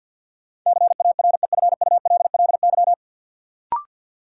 Morse code, heard as a keyed tone, about 48 wpm, 700 Hz. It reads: CAREFULLY
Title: Sentences from Top 1300 Words - New Word Drill Alt - 50wpm